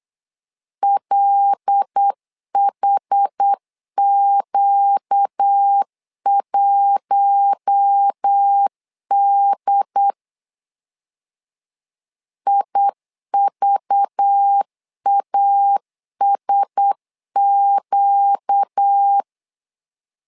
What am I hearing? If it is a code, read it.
Morse code, heard as a keyed tone, about 8 wpm, 800 Hz. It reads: LHQ1D IVASQ
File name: LEZIONI / 5 LBVC / 5 LBV 8wpm 2837